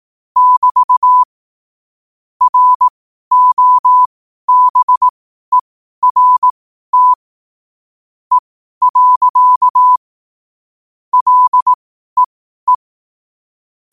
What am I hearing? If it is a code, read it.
Morse code, heard as a keyed tone, about 18 wpm, spaced as for 12 wpm, 1000 Hz. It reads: <BT> ROBERT E. LEE
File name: qotdfiles / 12wpm